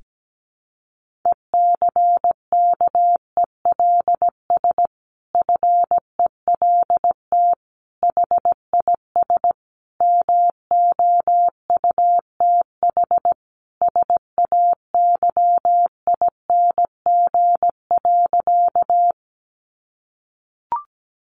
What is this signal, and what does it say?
Morse code, heard as a keyed tone, about 17 wpm, 700 Hz. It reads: ECKELS FELT HIS MOUTH SAYING.
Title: A Sound Of Thunder 17wpm